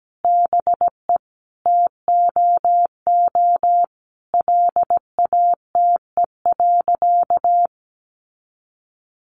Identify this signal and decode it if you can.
Morse code, heard as a keyed tone, about 17 wpm, 700 Hz. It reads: BE TOO LATE.
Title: Sentences from Top 300 Words 17wpm